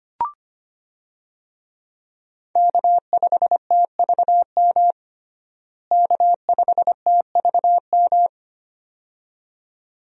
Morse code, heard as a keyed tone, about 25 wpm, 700 Hz. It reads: K5TVM K5TVM
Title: USA Call Signs - 5 Characters - Repeated 2x - 25wpm